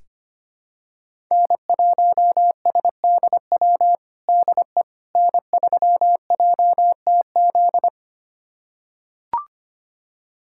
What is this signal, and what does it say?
Morse code, heard as a keyed tone, about 25 wpm, 700 Hz. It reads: N1SDW DE N3JTZ
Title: USA Callsign DE Callsign - 5 Characters - 25wpm